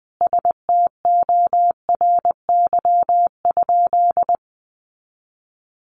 Morse code, heard as a keyed tone, about 20 wpm, 700 Hz. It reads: STORY?
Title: Sentences from Top 400 Words 20wpm